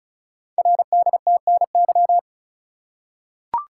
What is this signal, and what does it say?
Morse code, heard as a keyed tone, about 36 wpm, 700 Hz. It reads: RDTNY
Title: Code Groups - 5 Letters - 35wpm